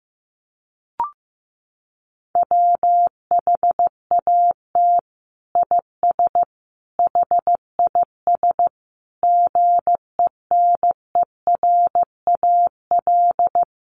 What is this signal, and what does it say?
Morse code, heard as a keyed tone, about 14 wpm, 700 Hz. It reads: WHAT IS HIS GENERAL
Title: Sentences from Top 1000 Words 15wpm